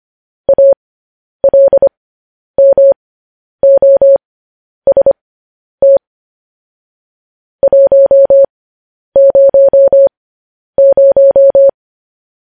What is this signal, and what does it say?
Morse code, heard as a keyed tone, about 25 wpm, spaced as for 10 wpm, 550 Hz. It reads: ALMOST 100
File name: podcast / News headlines-1x-25x10